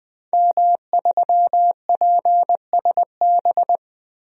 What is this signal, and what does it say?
Morse code, heard as a keyed tone, about 20 wpm, 700 Hz. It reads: M3PSB
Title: DXCC-100 Call Signs - 5 Characters - Repeated 3x - 20wpm